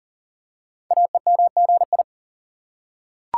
Morse code, heard as a keyed tone, about 38 wpm, 690 Hz. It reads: AEMGI
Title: Code Groups - 5 Letters - 40wpm